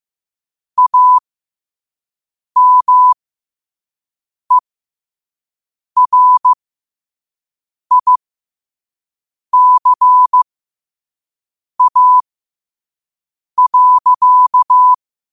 Morse code, heard as a keyed tone, about 15 wpm, spaced as for 5 wpm, 1000 Hz. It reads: AMERICA.
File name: 5wpm